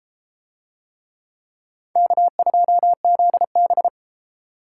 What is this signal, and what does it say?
Morse code, heard as a keyed tone, about 32 wpm, 700 Hz. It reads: K2ZB